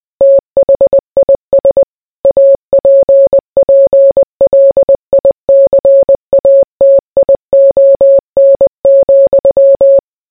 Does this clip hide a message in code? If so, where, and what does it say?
Yes: Morse code from 0.2 to 10.0 s, heard as a keyed tone, about 20 wpm, 550 Hz. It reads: THIS APPLICATION,